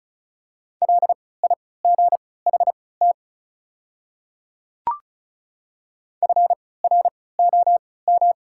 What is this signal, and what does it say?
Morse code, heard as a keyed tone, about 36 wpm, spaced as for 20 wpm, 700 Hz. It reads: LIGHT FROM
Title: 3x Character Spacing - Top 200 Words - 35wpm